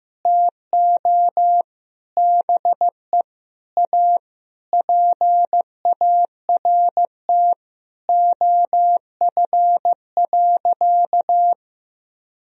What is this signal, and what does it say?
Morse code, heard as a keyed tone, about 16 wpm, 700 Hz. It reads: TO BE A PART OF.
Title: Sentences from Top 1000 Words - Review Part 1 - 15wpm